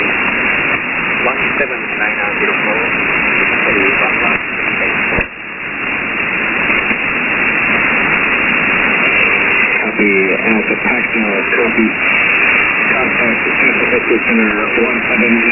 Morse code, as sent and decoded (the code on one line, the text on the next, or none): none